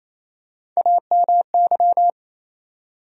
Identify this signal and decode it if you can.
Morse code, heard as a keyed tone, about 28 wpm, 700 Hz. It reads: AMY
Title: USA Common Names - Max 4 Letters - 28wpm